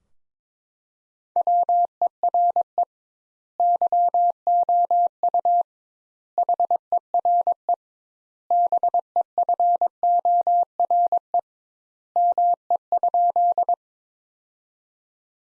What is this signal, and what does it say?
Morse code, heard as a keyed tone, about 22 wpm, 700 Hz. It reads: WERE YOU HERE BEFORE ME?
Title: Sentences Top 200 Words - 22wpm